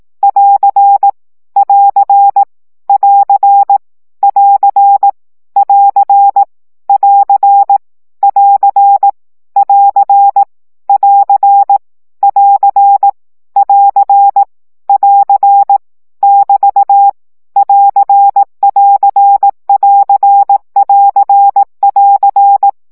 Morse code, heard as a keyed tone, about 18 wpm, 800 Hz. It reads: <AR> <AR> <AR> <AR> <AR> <AR> <AR> <AR> <AR> <AR> <AR> <AR> <BT> <AR><AR><AR><AR><AR>